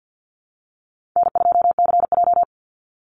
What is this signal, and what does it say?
Morse code, heard as a keyed tone, about 50 wpm, 700 Hz. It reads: N2CJ
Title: Easy Call Signs - 50wpm